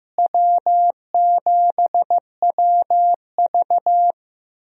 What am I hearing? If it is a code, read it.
Morse code, heard as a keyed tone, about 15 wpm, 700 Hz. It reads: W7WV
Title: USA Call Signs - 4 Characters - Repeated 3x - 15wpm